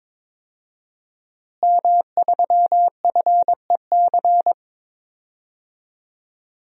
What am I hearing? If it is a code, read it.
Morse code, heard as a keyed tone, about 22 wpm, 700 Hz. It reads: M3FEC
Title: DXCC-100 Call Signs - 5 Characters - Repeated 2x - 22wpm